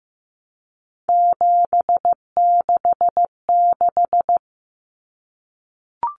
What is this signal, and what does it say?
Morse code, heard as a keyed tone, about 15 wpm, 700 Hz. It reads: 766
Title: Common Cut Numbers - 3-Digit 15wpm